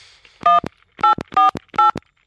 DTMF, heard as telephone touch tones, tones of about 130 ms. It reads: 1848